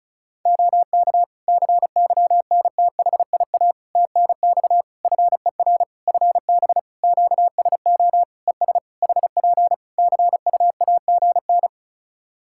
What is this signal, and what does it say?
Morse code, heard as a keyed tone, about 36 wpm, 700 Hz. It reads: OK CYNTHIA TNX FER FB QSO ES HP CUAGN
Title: K1USN SST QSO Practice 35wpm